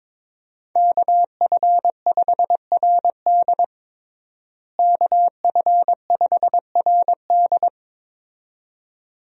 Morse code, heard as a keyed tone, about 21 wpm, 700 Hz. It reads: KF5RD KF5RD
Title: USA Call Signs - 5 Characters - Repeated 2x - 22wpm